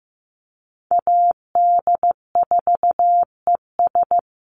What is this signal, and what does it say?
Morse code, heard as a keyed tone, about 15 wpm, 700 Hz. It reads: AD4ES
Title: Call Signs Repeated 3x - 15wpm